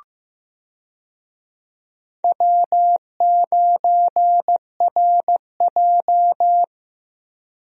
Morse code, heard as a keyed tone, about 15 wpm, 700 Hz. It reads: W9RJ
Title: USA Call Signs - 4 Characters - Repeated 3x - 15wpm